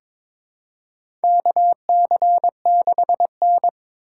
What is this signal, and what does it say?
Morse code, heard as a keyed tone, about 22 wpm, 700 Hz. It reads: KC6N
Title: USA Call Signs - 4 Characters - 4x Character Spacing - 22wpm